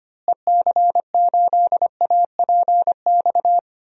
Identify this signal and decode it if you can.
Morse code, heard as a keyed tone, about 25 wpm, 700 Hz. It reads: EC8APX
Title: DXCC-100 Call Signs - 6 Characters - Repeated 3x - 25wpm